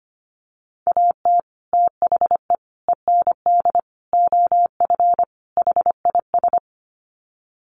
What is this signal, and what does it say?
Morse code, heard as a keyed tone, about 25 wpm, 700 Hz. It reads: AT THE END OF HIS
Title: Sets of 5 Words Encore 25wpm